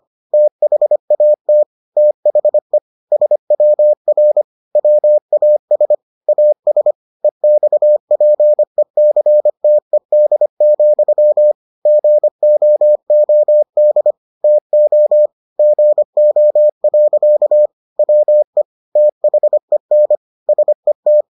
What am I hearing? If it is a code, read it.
Morse code, heard as a keyed tone, about 25 wpm, 600 Hz. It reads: THAT THE SWR WAS AS EXPECTED, GOOD TO GO. WE THEN SET